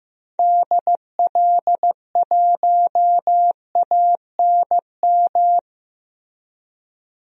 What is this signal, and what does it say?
Morse code, heard as a keyed tone, about 15 wpm, 700 Hz. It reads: DL1ANM
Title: DXCC-100 Call Signs - 6 Characters - 15wpm